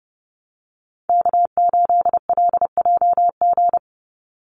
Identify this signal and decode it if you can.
Morse code, heard as a keyed tone, about 30 wpm, 700 Hz. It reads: K8LJG
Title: NA SOTA Chaser Callsigns 30wpm